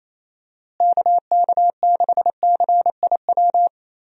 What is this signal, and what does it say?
Morse code, heard as a keyed tone, about 28 wpm, 700 Hz. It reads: KK6CIW
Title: USA Call Signs - 6 Characters - Repeated 2x - 28wpm